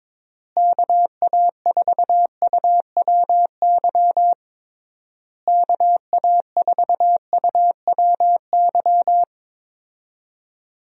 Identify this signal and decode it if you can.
Morse code, heard as a keyed tone, about 22 wpm, 700 Hz. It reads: KA4UWY KA4UWY
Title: USA Call Signs - 6 Characters - Repeated 3x - 22wpm